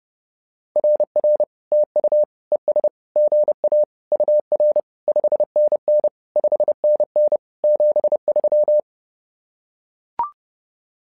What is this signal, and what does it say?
Morse code, heard as a keyed tone, about 30 wpm, 600 Hz. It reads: RR TU ES GA UR 5NN 5NN 73
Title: SOTA QSO Practice 30wpm